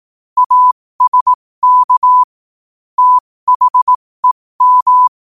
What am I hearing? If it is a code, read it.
Morse code, heard as a keyed tone, about 18 wpm, spaced as for 15 wpm, 1000 Hz. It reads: ASK THEM
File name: qotdfiles / 15wpm